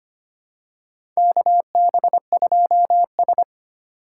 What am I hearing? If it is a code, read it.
Morse code, heard as a keyed tone, about 25 wpm, 700 Hz. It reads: KB2S
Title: K1USN SST QSO Practice - 25wpm